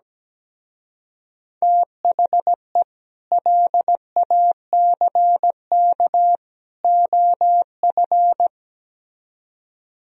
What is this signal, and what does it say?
Morse code, heard as a keyed tone, about 17 wpm, 700 Hz. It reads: THE LACK OF
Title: Speed-Racing - Sets of 3 Words 17wpm